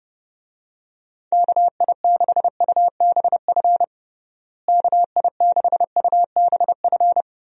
Morse code, heard as a keyed tone, about 30 wpm, 700 Hz. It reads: KI6UBF KI6UBF
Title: USA Call Signs - 6 Characters - Repeated 2x - 30wpm